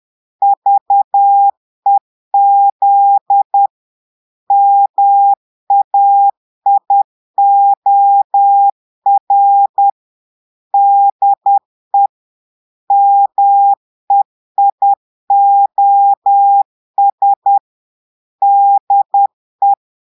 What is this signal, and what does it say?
Morse code, heard as a keyed tone, about 10 wpm, 800 Hz. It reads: VEZ MAIOR DE MEIOS DE
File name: portugais texte en clair 2 vitesse 10 mots minutes